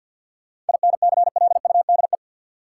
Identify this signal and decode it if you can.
Morse code, heard as a keyed tone, about 47 wpm, 700 Hz. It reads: INCLUDE
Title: Sentences from Top 1000 Words - Review Part 3 - New Word Drill - 50wpm